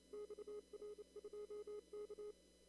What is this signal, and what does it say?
Morse code, heard as a keyed tone, about 28 wpm, 1250 Hz, with noise. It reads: XR2K
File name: CW / kt5j wpx cw-op-k5pi